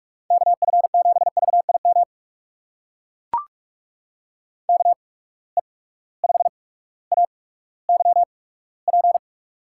Morse code, heard as a keyed tone, about 44 wpm, spaced as for 19 wpm, 700 Hz. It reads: KF7VIM KE5AYP
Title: USA Call Signs - 6 Characters - 8x Character Spacing - 45wpm